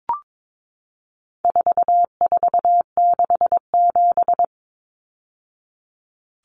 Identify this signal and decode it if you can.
Morse code, heard as a keyed tone, about 22 wpm, 700 Hz. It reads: 4467
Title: Numbers - 4-Digit 22wpm